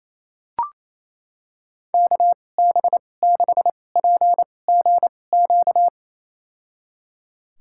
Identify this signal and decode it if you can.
Morse code, heard as a keyed tone, about 28 wpm, spaced as for 20 wpm, 690 Hz. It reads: KB6PGQ
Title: USA Call Signs - 6 Characters - 2x Character Spacing - 28wpm